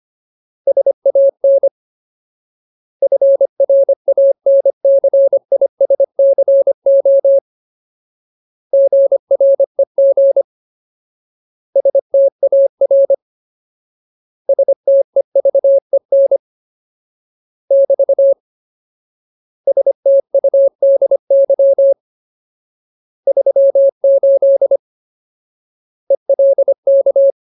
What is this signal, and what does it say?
Morse code, heard as a keyed tone, about 25 wpm, 550 Hz. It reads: SAN FRANCISCO GREG STAR STEVEN <BT> STUDY 38 ELK
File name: Ultimate Word List / 25WPM Ultimate